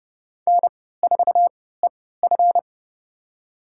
Morse code, heard as a keyed tone, about 29 wpm, spaced as for 17 wpm, 700 Hz. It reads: N4EF
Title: USA Call Signs - 4 Characters - 3x Character Spacing - 30wpm